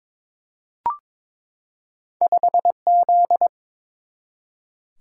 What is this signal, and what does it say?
Morse code, heard as a keyed tone, about 22 wpm, 700 Hz. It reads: 5Z
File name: practice / 22wpm / Two Letter-Number Combinations - 22wpm